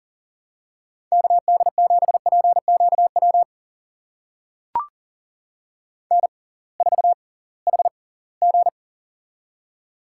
Morse code, heard as a keyed tone, about 40 wpm, spaced as for 23 wpm, 700 Hz. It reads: KD7PQW N4HG
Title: USA Call Signs - 6x Character Spacing - 40wpm